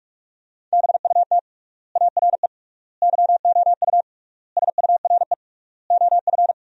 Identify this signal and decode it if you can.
Morse code, heard as a keyed tone, about 46 wpm, 700 Hz. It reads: BUT ARE YOU SURE OF
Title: 3x Word-Spacing - Sentences from Top 300 Words - 45wpm